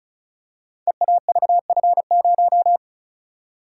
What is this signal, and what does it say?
Morse code, heard as a keyed tone, about 36 wpm, 700 Hz. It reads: EAVF0
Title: Code Groups - 5 Letter:Numbers - 35wpm